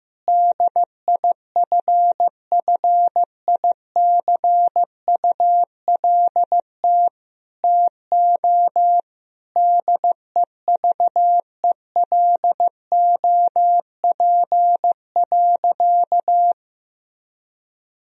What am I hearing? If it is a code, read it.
Morse code, heard as a keyed tone, about 15 wpm, 700 Hz. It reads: DIFFICULT TO DEVELOP.
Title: Sentences from Top 1000 Words - Review Part 2 - 15wpm